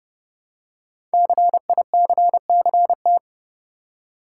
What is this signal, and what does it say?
Morse code, heard as a keyed tone, about 30 wpm, 700 Hz. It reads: CICCT
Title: Code Groups - 5 Letters - 30wpm